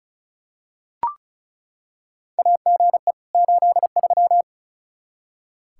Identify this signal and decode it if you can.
Morse code, heard as a keyed tone, about 34 wpm, 700 Hz. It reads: AGE 83
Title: K1USN SST QSO Practice 35wpm